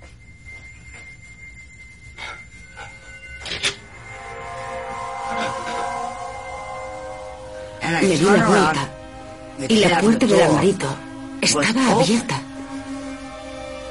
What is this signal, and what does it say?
Dial-up modem handshake, with speech.